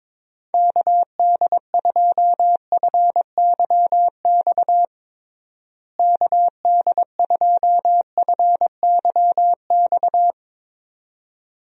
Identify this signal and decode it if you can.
Morse code, heard as a keyed tone, about 22 wpm, 700 Hz. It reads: KD2FYX KD2FYX
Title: USA Call Signs - 6 Characters - Repeated 2x - 22wpm